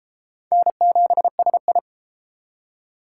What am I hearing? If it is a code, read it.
Morse code, heard as a keyed tone, about 28 wpm, 700 Hz. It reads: N7SI